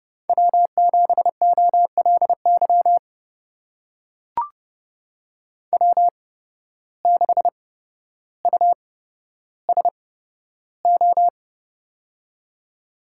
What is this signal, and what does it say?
Morse code, heard as a keyed tone, about 31 wpm, spaced as for 13 wpm, 700 Hz. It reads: W7OLY W6USO
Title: USA Call Signs - 5 Characters - 8x Character Spacing - 30wpm